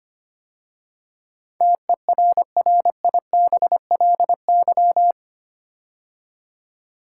Morse code, heard as a keyed tone, about 25 wpm, 700 Hz. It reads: TERRIBLY
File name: 8 Letter Words - First 3 Letters Given - 25wpm